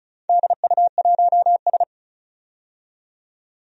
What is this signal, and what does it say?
Morse code, heard as a keyed tone, about 36 wpm, 690 Hz. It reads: DU1S
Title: DXCC-100 Call Signs - 4 Characters - 35wpm